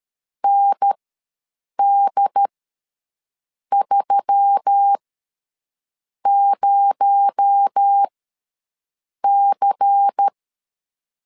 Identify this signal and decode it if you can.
Morse code, heard as a keyed tone, about 13 wpm, spaced as for 6 wpm, 800 Hz. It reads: ND30C
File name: 7 GF9Z ws 12wpm